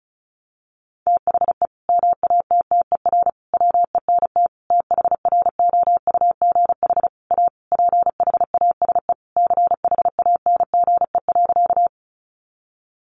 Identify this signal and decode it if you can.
Morse code, heard as a keyed tone, about 34 wpm, 700 Hz. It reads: THE MATTER WENT THROUGH A PHASE CHANGE.
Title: Sentences from Top 1200 Words 35wpm